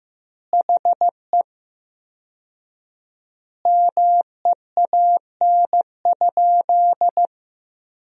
Morse code, heard as a keyed tone, about 15 wpm, 700 Hz. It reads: HE MEAN?
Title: 4x Word-Spacing - Sentences from Top 200 Words - 15wpm